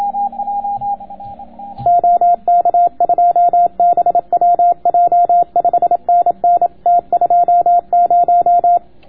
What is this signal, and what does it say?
Morse code, heard as a keyed tone, about 27 wpm, 700 Hz, with radio noise. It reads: OK2BWJ 5NN T20